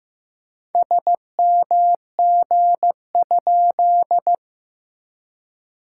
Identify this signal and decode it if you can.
Morse code, heard as a keyed tone, about 15 wpm, 700 Hz. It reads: SMG?
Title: SOTA QSO Practice 15wpm